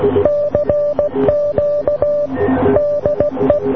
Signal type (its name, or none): Morse code